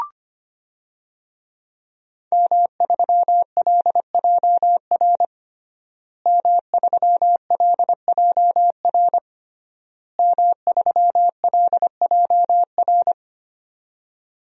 Morse code, heard as a keyed tone, about 25 wpm, 700 Hz. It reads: M3LJR M3LJR M3LJR
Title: DXCC-100 Call Signs - 5 Characters - Repeated 3x - 25wpm